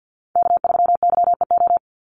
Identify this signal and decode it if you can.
Morse code, heard as a keyed tone, about 49 wpm, 700 Hz. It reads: K3YEO
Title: Call Signs Repeated 2x - 50wpm